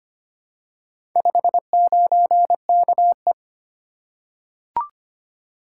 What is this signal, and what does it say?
Morse code, heard as a keyed tone, about 25 wpm, 700 Hz. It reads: 59KE
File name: practice / 25wpm / Four Letter-Number Combinations - 25wpm